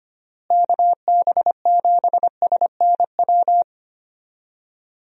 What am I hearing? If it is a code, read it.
Morse code, heard as a keyed tone, about 25 wpm, 700 Hz. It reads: KB7SNW
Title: USA Call Signs - 6 Characters - Repeated 3x - 25wpm